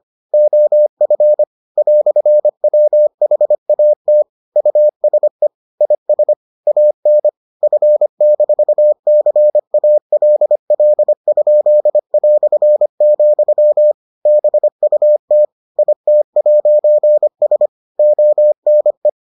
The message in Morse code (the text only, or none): OF "WHAT USE IS AN F-CALL?", BUT IT'S ONE